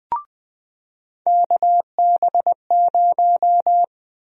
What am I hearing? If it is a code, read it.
Morse code, heard as a keyed tone, about 21 wpm, 700 Hz. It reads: KB0
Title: USA Call Sign Prefixes - 20wpm